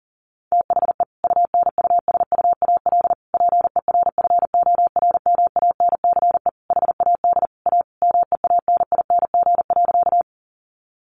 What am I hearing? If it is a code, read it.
Morse code, heard as a keyed tone, about 40 wpm, 700 Hz. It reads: THE UNUSUAL PERFORMANCE HAD A MEANING.
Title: Sentences from Top 1400 Words 40wpm